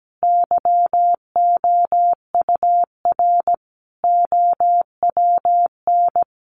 Morse code, heard as a keyed tone, about 17 wpm, 700 Hz. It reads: YOUR OWN
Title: Sentences from Top 1500 Words 17wpm